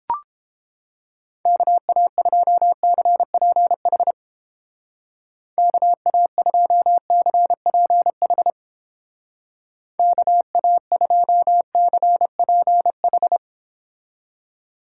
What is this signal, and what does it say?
Morse code, heard as a keyed tone, about 29 wpm, 700 Hz. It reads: KA2CPH KA2CPH KA2CPH